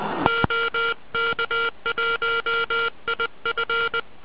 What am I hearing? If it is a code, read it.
Morse code, heard as a keyed tone, about 20 wpm, 1390 Hz. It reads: OK1IF